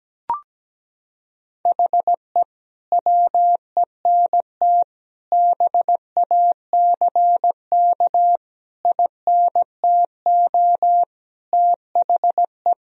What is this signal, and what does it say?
Morse code, heard as a keyed tone, about 17 wpm, 700 Hz. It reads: HE WENT BACK INTO THE
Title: Sentences from Top 1000 Words - Review Part 2 - 17wpm